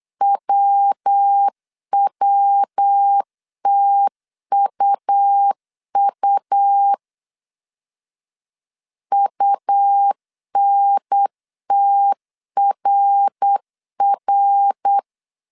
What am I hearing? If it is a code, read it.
Morse code, heard as a keyed tone, about 8 wpm, 800 Hz. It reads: WWTUU UNTRR
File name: CW / DET / 1 3 1500C S B 8wpm 4141